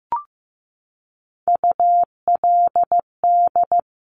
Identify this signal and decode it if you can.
Morse code, heard as a keyed tone, about 15 wpm, 690 Hz. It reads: ULD
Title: Top Three Letter Combinations in Words 15wpm